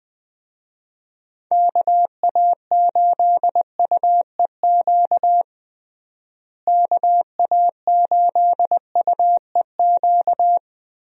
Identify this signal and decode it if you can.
Morse code, heard as a keyed tone, about 20 wpm, 700 Hz. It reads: KA8UEQ KA8UEQ
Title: USA Call Signs - 6 Characters - Repeated 3x - 20wpm